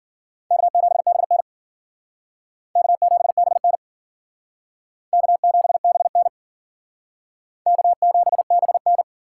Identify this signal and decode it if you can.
Morse code, heard as a keyed tone, about 54 wpm, 700 Hz. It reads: K7BN K7BN K7BN K7BN